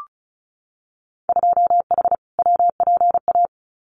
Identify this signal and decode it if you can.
Morse code, heard as a keyed tone, about 34 wpm, 700 Hz. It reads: 2H WPA
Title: Winter Field Day Exchanges 35wpm